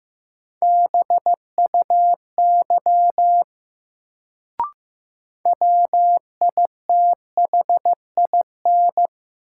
Morse code, heard as a keyed tone, about 16 wpm, 700 Hz. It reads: BUY WITHIN
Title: Sentences from Top 700 Words - New Word Drill - 15wpm